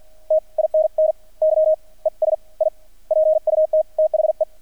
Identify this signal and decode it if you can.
Morse code, heard as a keyed tone, about 50 wpm, spaced as for 38 wpm, 630 Hz. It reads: TINNY ES I PUT THE
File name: archived / CRSnet